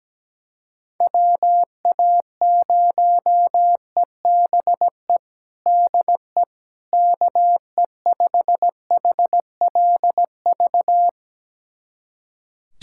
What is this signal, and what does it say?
Morse code, heard as a keyed tone, about 17 wpm, 700 Hz. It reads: WA0EBE DE KE5HLV
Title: USA Callsign DE Callsign - 6 Characters - 17wpm